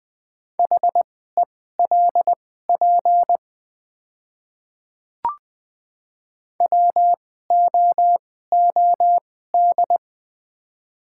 Morse code, heard as a keyed tone, about 21 wpm, spaced as for 15 wpm, 700 Hz. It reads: HELP WOOD
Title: 2x Character Spacing - Top 500 Words - 20wpm